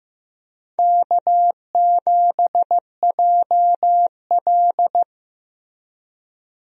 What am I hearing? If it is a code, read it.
Morse code, heard as a keyed tone, about 16 wpm, 700 Hz. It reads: K7JL
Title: USA Call Signs - 4 Characters - Repeated 2x - 15wpm